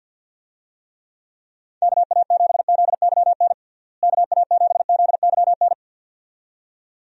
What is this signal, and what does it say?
Morse code, heard as a keyed tone, about 52 wpm, 700 Hz. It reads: KA7ZYN KA7ZYN
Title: USA Call Signs - 6 Characters - Repeated 2x - 50wpm